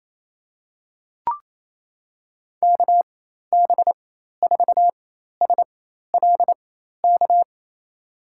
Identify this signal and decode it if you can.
Morse code, heard as a keyed tone, about 28 wpm, spaced as for 13 wpm, 700 Hz. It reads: KB4SLK
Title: USA Call Signs - 6 Characters - 4x Character Spacing - 28wpm